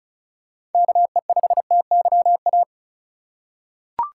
Morse code, heard as a keyed tone, about 34 wpm, 700 Hz. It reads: KE5TYA
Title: USA Call Signs - 4x Character Spacing - 35wpm